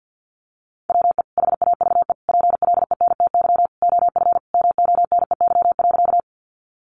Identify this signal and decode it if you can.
Morse code, heard as a keyed tone, about 50 wpm, 700 Hz. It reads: WE HAVE PLENTY OF MONEY.
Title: Sentences from Top 1300 Words 50wpm